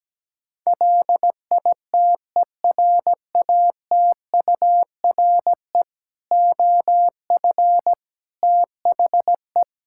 Morse code, heard as a keyed tone, about 17 wpm, 700 Hz. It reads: LITERATURE OF THE
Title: Sentences from Top 1000 Words - Review Part 5 - 17wpm